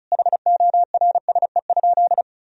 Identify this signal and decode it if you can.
Morse code, heard as a keyed tone, about 36 wpm, 700 Hz. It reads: HORSE?
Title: Sentences from Top 1000 Words - Review Part 2 - 35wpm